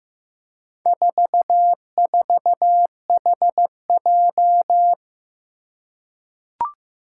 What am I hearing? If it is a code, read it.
Morse code, heard as a keyed tone, about 15 wpm, 700 Hz. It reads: 44HJ
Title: Four Challenging Letter-Number Combinations 15wpm